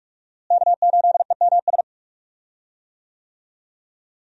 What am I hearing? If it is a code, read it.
Morse code, heard as a keyed tone, about 42 wpm, 700 Hz. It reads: K8EMS
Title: USA Call Signs - 5 Characters - Repeated 3x - 45wpm